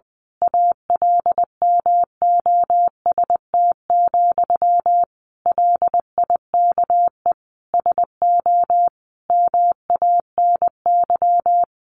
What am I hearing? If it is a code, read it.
Morse code, heard as a keyed tone, about 20 wpm, 700 Hz. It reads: ALMOST, LIKE SO MANY
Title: A Sound Of Thunder 20wpm